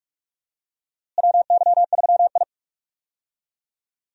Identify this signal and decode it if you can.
Morse code, heard as a keyed tone, about 46 wpm, 700 Hz. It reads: WY3I
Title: USA Call Signs - 4 Characters - Repeated 3x - 45wpm